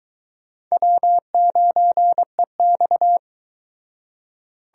Morse code, heard as a keyed tone, about 22 wpm, 700 Hz. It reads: W9EX